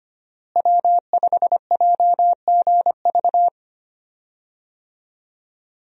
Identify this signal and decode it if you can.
Morse code, heard as a keyed tone, about 25 wpm, 700 Hz. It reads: W5JGV
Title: USA Call Signs - 5 Characters - Repeated 2x - 25wpm